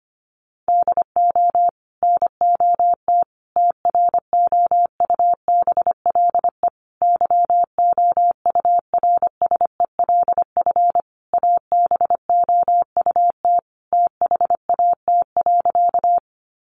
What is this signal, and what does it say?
Morse code, heard as a keyed tone, about 25 wpm, 700 Hz. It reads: DO NOT TROUBLE YOURSELF ABOUT THAT.
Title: Sentences from Top 1200 Words 25wpm